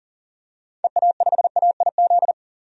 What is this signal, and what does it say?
Morse code, heard as a keyed tone, about 40 wpm, 700 Hz. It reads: EA5AIZ
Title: DXCC-100 Call Signs - 6 Characters - 40wpm